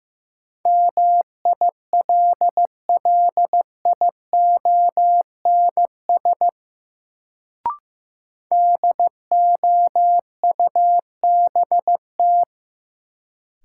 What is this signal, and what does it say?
Morse code, heard as a keyed tone, about 16 wpm, 700 Hz. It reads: MILLIONS DOUBT
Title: Sets of 5 Words - New Word Drill - 15wpm